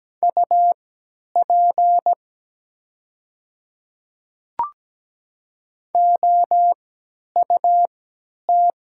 Morse code, heard as a keyed tone, about 17 wpm, spaced as for 10 wpm, 690 Hz. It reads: UP OUT
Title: 3x Character Spacing - Top 100 Words - 17wpm